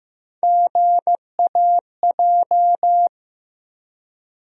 Morse code, heard as a keyed tone, about 15 wpm, 700 Hz. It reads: GAJ
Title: USA Call Sign Postfixes - 3 Letters 15wpm